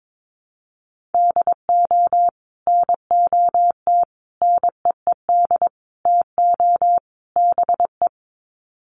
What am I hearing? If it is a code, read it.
Morse code, heard as a keyed tone, about 22 wpm, 700 Hz. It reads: DO NOT NEED TO BE
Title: Sets of 5 Words Encore 22wpm